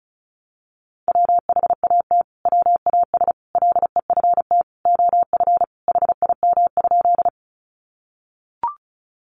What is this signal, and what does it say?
Morse code, heard as a keyed tone, about 34 wpm, 700 Hz. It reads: WHAT WAS LEFT OF HIM?
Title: Sentences from Top 300 Words 35wpm